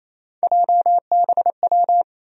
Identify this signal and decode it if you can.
Morse code, heard as a keyed tone, about 28 wpm, 700 Hz. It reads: JBW
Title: USA Call Sign Postfixes - 3 Letters 28wpm